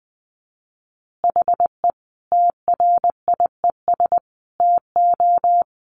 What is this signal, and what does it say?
Morse code, heard as a keyed tone, about 20 wpm, 700 Hz. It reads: HE TRIES TO
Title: Easy Phrases 20wpm